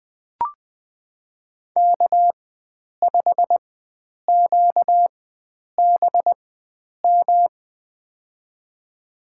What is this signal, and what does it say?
Morse code, heard as a keyed tone, about 21 wpm, spaced as for 9 wpm, 700 Hz. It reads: K5QBM
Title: USA Call Signs - 5 Characters - 4x Character Spacing - 20wpm